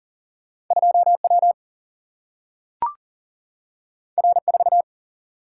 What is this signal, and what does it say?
Morse code, heard as a keyed tone, about 39 wpm, 700 Hz. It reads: 2 W R 4
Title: Single-Letter Single-Number 40wpm